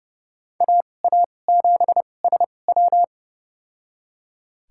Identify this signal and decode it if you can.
Morse code, heard as a keyed tone, about 32 wpm, spaced as for 22 wpm, 700 Hz. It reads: AA7SW
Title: USA Call Signs - 5 Characters - 2x Character Spacing - 30wpm